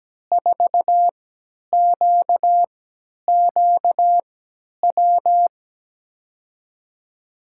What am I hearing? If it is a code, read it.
Morse code, heard as a keyed tone, about 17 wpm, spaced as for 10 wpm, 700 Hz. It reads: 4QQW
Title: USA Call Signs - 5 Characters - 3x Character Spacing - 17wpm